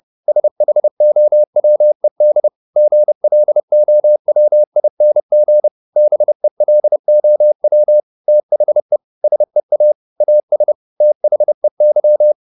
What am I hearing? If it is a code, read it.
Morse code, heard as a keyed tone, about 30 wpm, 600 Hz. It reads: SHOWED GLOWING BELOW THE SEA AS THEY